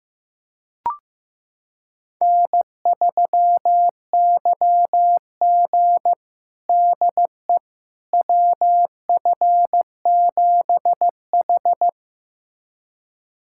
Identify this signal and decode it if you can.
Morse code, heard as a keyed tone, about 16 wpm, 700 Hz. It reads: N3YG DE WF7H
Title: USA Callsign DE Callsign - 4 Characters - 15wpm